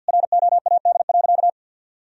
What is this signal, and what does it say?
Morse code, heard as a keyed tone, about 48 wpm, 700 Hz. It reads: ROAD.
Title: Sentences from Top 400 Words and Abbreviations 50wpm